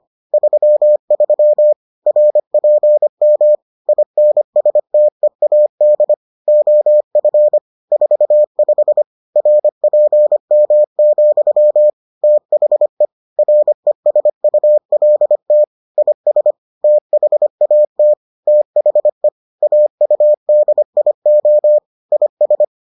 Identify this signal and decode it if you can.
Morse code, heard as a keyed tone, about 25 wpm, 600 Hz. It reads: 33 RPM INSTEAD OF 45 RPM, THE RESULT IS THAT THE AUDIO IS